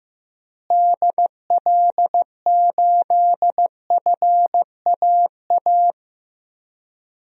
Morse code, heard as a keyed tone, about 15 wpm, 700 Hz. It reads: DL8FAA
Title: DXCC-100 Call Signs - 6 Characters - Repeated 2x - 15wpm